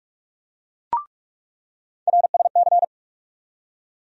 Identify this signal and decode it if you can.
Morse code, heard as a keyed tone, about 43 wpm, 700 Hz. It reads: RSC
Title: USA Call Sign Postfixes - 3 Letters 45wpm